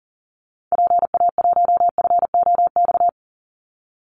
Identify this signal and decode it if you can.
Morse code, heard as a keyed tone, about 40 wpm, 700 Hz. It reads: PA1FOX
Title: Call Signs - 40wpm